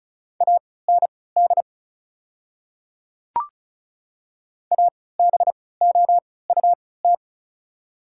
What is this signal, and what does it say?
Morse code, heard as a keyed tone, about 36 wpm, spaced as for 20 wpm, 700 Hz. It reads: AND ABOUT
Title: 3x Character Spacing - Top 500 Words - 35wpm